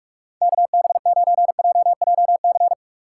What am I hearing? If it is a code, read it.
Morse code, heard as a keyed tone, about 43 wpm, 700 Hz. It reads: KB9JJC
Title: USA Call Signs - 2x Character Spacing - 45wpm